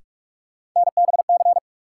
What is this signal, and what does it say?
Morse code, heard as a keyed tone, about 42 wpm, 700 Hz. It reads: NBC
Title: Sets of 5 Words Encore - New Word Drill - 45wpm